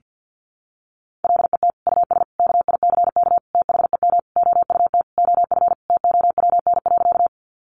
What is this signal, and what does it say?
Morse code, heard as a keyed tone, about 50 wpm, 700 Hz. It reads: LET US KICK THEM OUT OF TOWN.
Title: Sentences from Top 1300 Words 50wpm